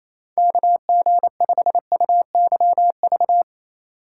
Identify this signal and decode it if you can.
Morse code, heard as a keyed tone, about 34 wpm, 700 Hz. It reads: KG5UYV
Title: Speed Racing - USA 2x3 Call Signs 28wpm